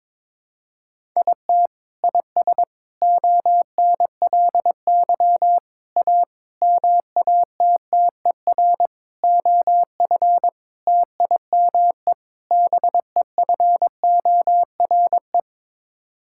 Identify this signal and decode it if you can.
Morse code, heard as a keyed tone, about 22 wpm, 700 Hz. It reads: IT IS ONLY A MATTER OF TIME BEFORE
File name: practice / 22wpm / Sets of 8 Words - 22wpm